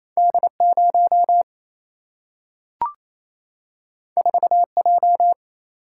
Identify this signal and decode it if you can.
Morse code, heard as a keyed tone, about 28 wpm, 700 Hz. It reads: D 0 4 J
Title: Single-Letter Single-Number 28wpm